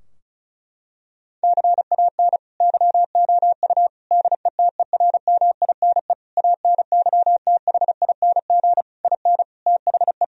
Morse code, heard as a keyed tone, about 36 wpm, 700 Hz. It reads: CAN YOU DETERMINE ANYTHING IN THE
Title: Sentences from Top 1000 Words - Review Part 4 - 35wpm